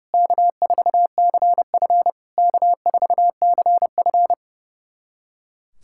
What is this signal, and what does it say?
Morse code, heard as a keyed tone, about 30 wpm, 690 Hz. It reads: K4CF K4CF
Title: SOTA QSO Practice 30wpm